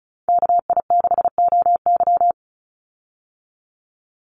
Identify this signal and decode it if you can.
Morse code, heard as a keyed tone, about 34 wpm, 700 Hz. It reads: KI6OY
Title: Call Signs Repeated 2x - 35wpm